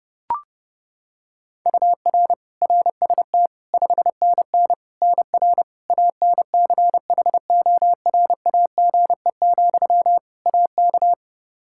Morse code, heard as a keyed tone, about 32 wpm, 700 Hz. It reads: UR RST 5NN NR ANCHORAGE, AK
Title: Practice QSO - Exchange 1 - Beginning - 30wpm